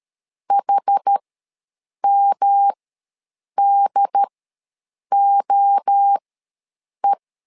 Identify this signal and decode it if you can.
Morse code, heard as a keyed tone, about 13 wpm, spaced as for 7 wpm, 800 Hz. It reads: HMDOE
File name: LEZIONI / 3 MWRO / 3 MWRO ws 12wpm 1914